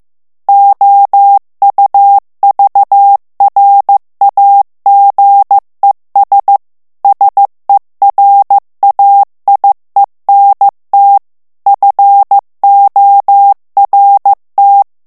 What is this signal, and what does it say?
Morse code, heard as a keyed tone, about 15 wpm, 800 Hz. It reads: OUVRAGES SERAIENT FORT
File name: morse mp3 / FRANCAIS TEXTE CLAIR 2 VITESSE 14 MOTS MINUTES